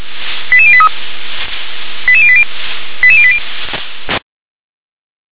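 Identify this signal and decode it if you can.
Five-tone selective call (ZVEI), heard as quick runs of five tones, tones of about 70 ms. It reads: 88083 88080 88080
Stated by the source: ZVEI five-tone selective call 88136 received via SDR.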